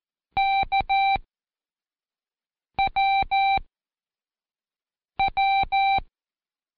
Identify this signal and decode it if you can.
Morse code, heard as a keyed tone, about 13 wpm, 780 Hz. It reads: K W W
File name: Code Quick / AudioDownld1 / Seg06-Day1-Intro K